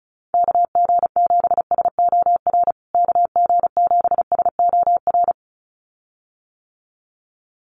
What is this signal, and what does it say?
Morse code, heard as a keyed tone, about 34 wpm, 700 Hz. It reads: KG7SOR KG7SOR
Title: USA Call Signs Repeated 2x 35wpm